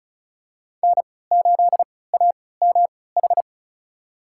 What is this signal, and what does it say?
Morse code, heard as a keyed tone, about 34 wpm, spaced as for 20 wpm, 700 Hz. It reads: N8AMH
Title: USA Call Signs - 3x Character Spacing - 35wpm